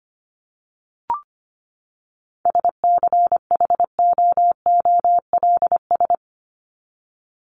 Morse code, heard as a keyed tone, about 25 wpm, 700 Hz. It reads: SCHOOLS
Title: ES:S Suffix - Top 100 Words with Base Words 25wpm